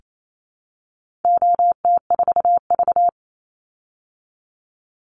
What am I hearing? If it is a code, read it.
Morse code, heard as a keyed tone, about 28 wpm, 700 Hz. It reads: OT4V
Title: Call Signs Repeated 2x - 28wpm